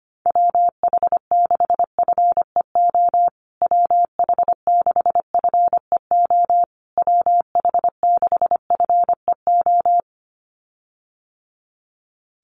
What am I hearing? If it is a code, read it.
Morse code, heard as a keyed tone, about 25 wpm, 700 Hz. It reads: WH6FEO WH6FEO WH6FEO
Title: USA Call Signs Repeated 3x 25wpm